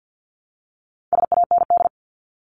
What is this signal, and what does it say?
Morse code, heard as a keyed tone, about 48 wpm, 700 Hz. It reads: SAND
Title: Top 1501-1750 Words 50wpm